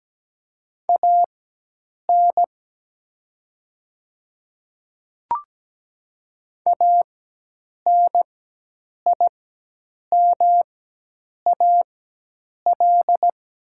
Morse code, heard as a keyed tone, about 17 wpm, spaced as for 8 wpm, 700 Hz. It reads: AN ANIMAL
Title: 4x Character Spacing - Top 200 Words - 17wpm